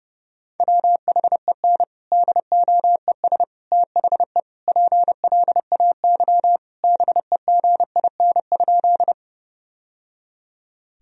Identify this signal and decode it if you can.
Morse code, heard as a keyed tone, about 30 wpm, 700 Hz. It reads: WHEN DOES THE PLAY BEGIN?
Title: Sentences from Top 1000 Words - Review Part 1 - 30wpm